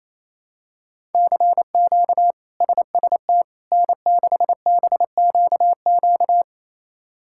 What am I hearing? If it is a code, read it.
Morse code, heard as a keyed tone, about 28 wpm, 700 Hz. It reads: CQ SST N6BQQ
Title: K1USN SST QSO Practice - 28wpm